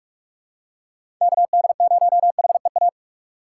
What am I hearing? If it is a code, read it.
Morse code, heard as a keyed tone, about 46 wpm, 690 Hz. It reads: KD0HEA
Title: USA Call Signs - 6 Characters - 3x Character Spacing - 45wpm